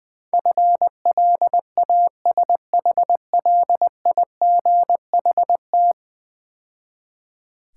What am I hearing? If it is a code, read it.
Morse code, heard as a keyed tone, about 20 wpm, 700 Hz. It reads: FLASHLIGHT
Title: Difficult Words - Speed-Racing with Hints - 20wpm